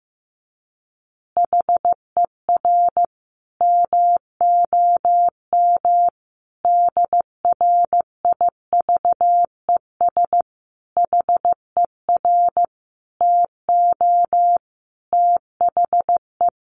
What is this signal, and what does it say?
Morse code, heard as a keyed tone, about 15 wpm, 700 Hz. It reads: HER MOM DRIVES HER TO THE
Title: Easy Phrases 15wpm